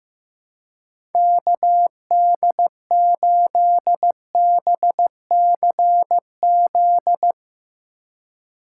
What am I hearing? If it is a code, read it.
Morse code, heard as a keyed tone, about 15 wpm, 700 Hz. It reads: KD8BCZ
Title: Speed Racing - USA Call Signs 15wpm